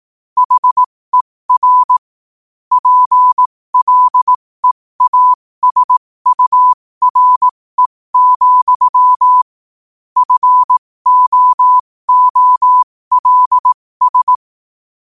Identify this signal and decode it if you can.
Morse code, heard as a keyed tone, about 18 wpm, spaced as for 15 wpm, 1000 Hz. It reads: HER PLEASURE, FOOLS
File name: qotdfiles / 15wpm